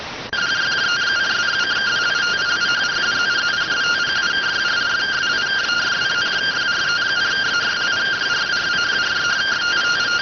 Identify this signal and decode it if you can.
Two-tone teleprinter signal, about 170 Hz shift.